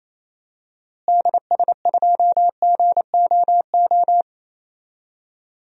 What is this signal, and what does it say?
Morse code, heard as a keyed tone, about 28 wpm, 700 Hz. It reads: DS2GOO
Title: DXCC-100 Call Signs - 6 Characters - Repeated 3x - 28wpm